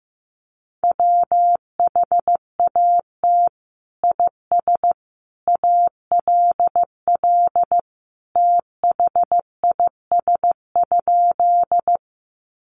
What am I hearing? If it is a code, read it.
Morse code, heard as a keyed tone, about 15 wpm, 700 Hz. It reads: WHAT IS ALL THIS?
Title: Sentences from Top 100 Words 15wpm